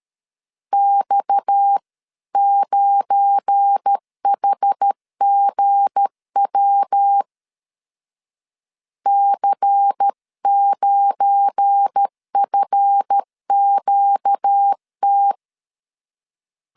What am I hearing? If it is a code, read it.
Morse code, heard as a keyed tone, about 13 wpm, 800 Hz. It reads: X9HGW C9FQT
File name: LEZIONI / 7 GF9Z / 7 GF9 12wpm 2319